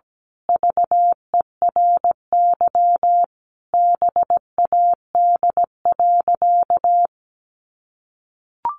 Morse code, heard as a keyed tone, about 17 wpm, 700 Hz. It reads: VERY BAD.
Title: Sentences from Top 1200 Words 17wpm